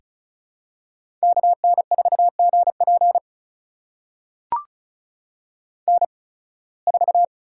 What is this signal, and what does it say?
Morse code, heard as a keyed tone, about 34 wpm, spaced as for 23 wpm, 700 Hz. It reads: KN4GP N4
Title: USA Call Signs - 5 Characters - 8x Character Spacing - 35wpm